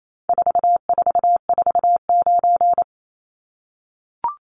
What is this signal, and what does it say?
Morse code, heard as a keyed tone, about 28 wpm, 700 Hz. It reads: 4449
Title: Numbers - 4-Digit 28wpm